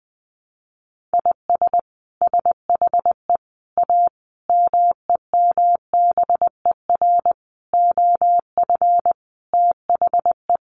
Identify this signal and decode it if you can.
Morse code, heard as a keyed tone, about 20 wpm, 700 Hz. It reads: IS SHE A MEMBER OF THE
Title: Sentences from Top 1100 Words 20wpm